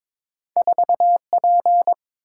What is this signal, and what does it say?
Morse code, heard as a keyed tone, about 22 wpm, 700 Hz. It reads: 4P
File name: practice / 22wpm / Two Letter-Number Combinations - 22wpm